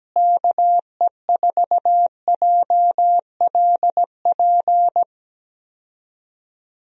Morse code, heard as a keyed tone, about 17 wpm, 700 Hz. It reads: KE4JLP
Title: USA Call Signs - 6 Characters - Repeated 3x - 17wpm